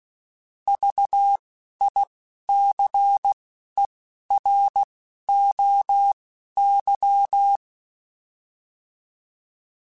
Morse code, heard as a keyed tone, about 16 wpm, spaced as for 11 wpm, 800 Hz. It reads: VICEROY
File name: Alphabetical / Alphabetical mp3s / Vs @10